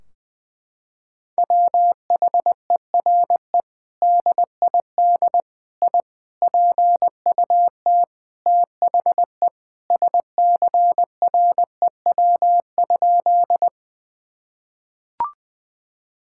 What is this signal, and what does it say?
Morse code, heard as a keyed tone, about 21 wpm, 700 Hz. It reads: WHERE DID I PUT THE SCREW?
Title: Sentences from Top 1600 Words 20wpm